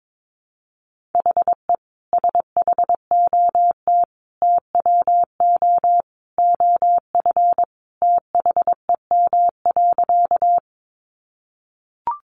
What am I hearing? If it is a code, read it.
Morse code, heard as a keyed tone, about 22 wpm, 700 Hz. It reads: HE SHOT TWO OF THEM.
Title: Sentences from Top 1100 Words 22wpm